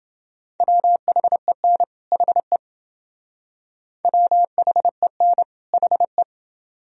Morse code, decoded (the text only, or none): WHEN HE WHEN HE